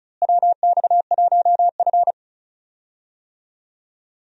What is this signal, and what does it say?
Morse code, heard as a keyed tone, about 36 wpm, 700 Hz. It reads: WX1F
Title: USA Call Signs - 4 Characters - Repeated 3x - 35wpm